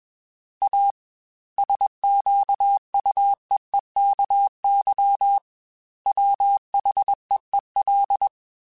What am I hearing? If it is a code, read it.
Morse code, heard as a keyed tone, about 21 wpm, 800 Hz. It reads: A SQUEEKY WHEEL
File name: Chuckles / Chuckles-mp3s / Squeaky @20